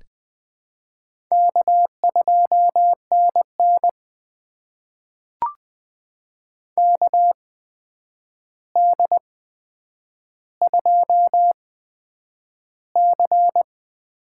Morse code, heard as a keyed tone, about 20 wpm, spaced as for 9 wpm, 700 Hz. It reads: K2NN KD2C
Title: USA Call Signs - 4 Characters - 8x Character Spacing - 20wpm